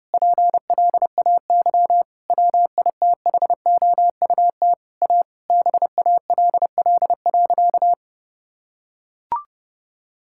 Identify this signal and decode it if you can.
Morse code, heard as a keyed tone, about 32 wpm, 700 Hz. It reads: PLAY WITHOUT A BALL.
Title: Sentences from Top 1000 Words - Review Part 6 - 30wpm